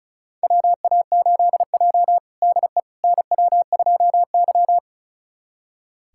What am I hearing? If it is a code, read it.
Morse code, heard as a keyed tone, about 34 wpm, 700 Hz. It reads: WA8J DE NW2Y
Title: USA Callsign DE Callsign - 4 Characters - 35wpm